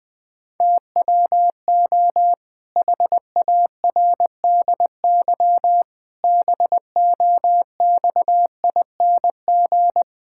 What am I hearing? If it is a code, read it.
Morse code, heard as a keyed tone, about 20 wpm, 700 Hz. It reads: TWO HARDY BOXING
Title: Pangram Challenge v2 - 20wpm